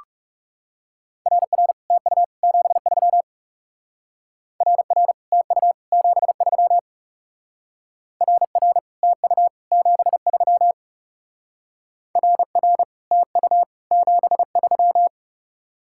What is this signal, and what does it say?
Morse code, heard as a keyed tone, about 40 wpm, 700 Hz. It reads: RR TU 73 RR TU 73 RR TU 73 RR TU 73